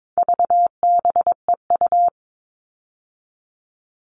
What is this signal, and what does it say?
Morse code, heard as a keyed tone, about 22 wpm, 700 Hz. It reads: VBEU
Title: All Cut Numbers - 4-Digit 22wpm